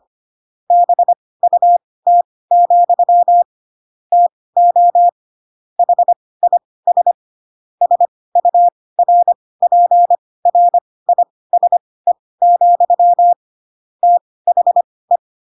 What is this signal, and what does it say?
Morse code, heard as a keyed tone, about 25 wpm, spaced as for 18 wpm, 700 Hz. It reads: BUT, TO HIS SURPRISE, THE
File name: Pinocchio / Pinocchio ch5 12 25 ik1hge